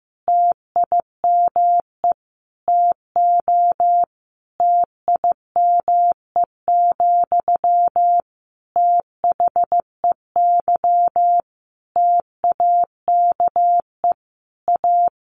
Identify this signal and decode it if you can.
Morse code, heard as a keyed tone, about 15 wpm, 700 Hz. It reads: TIME TO TIME, THEY TAKE A
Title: Sentences from Top 1300 Words 15wpm